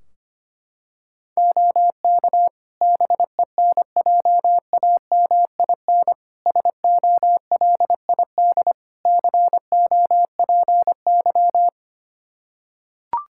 Morse code, heard as a keyed tone, about 25 wpm, 700 Hz. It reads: OK BENJAMIN SOLID COPY
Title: K1USN SST QSO Practice 25wpm